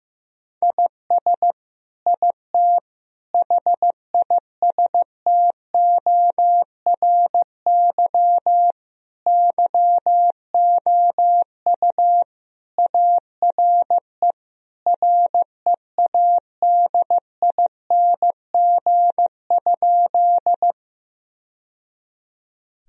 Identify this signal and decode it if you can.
Morse code, heard as a keyed tone, about 15 wpm, 700 Hz. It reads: IS IT HISTORY YOU ARE READING?
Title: Sentences from Top 1000 Words - Review Part 3 - 15wpm